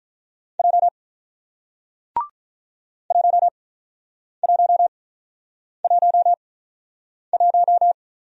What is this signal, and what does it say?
Morse code, heard as a keyed tone, about 46 wpm, spaced as for 9 wpm, 700 Hz. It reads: J 1111